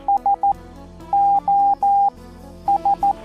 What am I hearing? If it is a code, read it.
Morse code, heard as a keyed tone, about 11 wpm, spaced as for 9 wpm, 810 Hz. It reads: SOS
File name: La-radiotelegrafia-Marittima-Puntata3